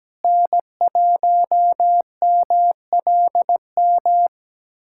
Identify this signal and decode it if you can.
Morse code, heard as a keyed tone, about 17 wpm, 700 Hz. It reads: N1MLM
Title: SOTA QSO Practice 17wpm